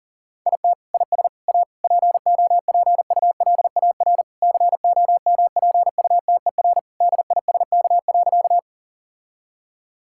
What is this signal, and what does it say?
Morse code, heard as a keyed tone, about 40 wpm, 700 Hz. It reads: IT IS A POPULAR COMPUTER DISK.
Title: Sentences from Top 1000 Words - Review Part 6 - 40wpm